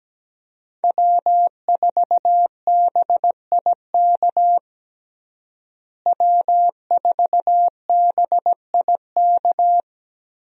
Morse code, heard as a keyed tone, about 17 wpm, 700 Hz. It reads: W4BIK W4BIK
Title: USA Call Signs - 5 Characters - Repeated 2x - 17wpm